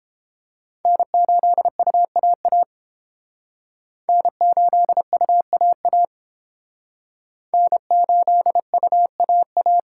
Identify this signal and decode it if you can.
Morse code, heard as a keyed tone, about 29 wpm, 700 Hz. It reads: N8UAA N8UAA N8UAA